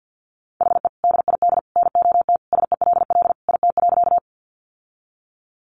Morse code, heard as a keyed tone, about 50 wpm, 700 Hz. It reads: HE DID NOT SELL IT.
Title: Sentences from Top 1100 Words 50wpm